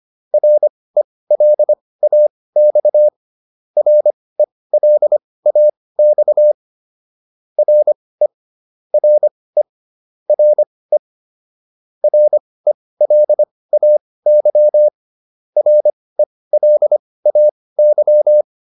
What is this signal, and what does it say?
Morse code, heard as a keyed tone, about 25 wpm, spaced as for 18 wpm, 600 Hz. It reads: RELAX RELAX RE RE RE RELAY RELAY